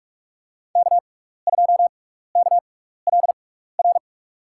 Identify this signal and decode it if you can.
Morse code, heard as a keyed tone, about 47 wpm, spaced as for 16 wpm, 690 Hz. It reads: K2KLR
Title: USA Call Signs - 5 Characters - 6x Character Spacing - 45wpm